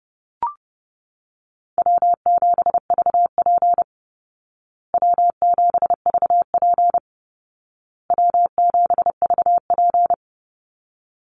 Morse code, heard as a keyed tone, about 30 wpm, 700 Hz. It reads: W7VP W7VP W7VP
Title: Call Signs Repeated 2x - 30wpm